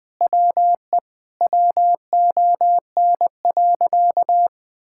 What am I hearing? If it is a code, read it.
Morse code, heard as a keyed tone, about 21 wpm, 700 Hz. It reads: WE WON.
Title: Sentences with 9-Word Phrases - Spoken Beginning - 20wpm